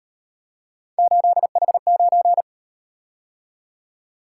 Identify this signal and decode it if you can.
Morse code, heard as a keyed tone, about 38 wpm, 700 Hz. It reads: 8H9